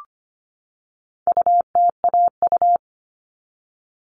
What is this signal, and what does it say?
Morse code, heard as a keyed tone, about 25 wpm, 700 Hz. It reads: UTAU
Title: All Cut Numbers - 4-Digit 25wpm